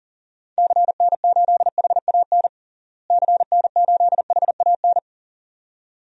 Morse code, heard as a keyed tone, about 40 wpm, 700 Hz. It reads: CN8HAN CN8HAN
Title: DXCC-100 Call Signs - 6 Characters - Repeated 2x - 40wpm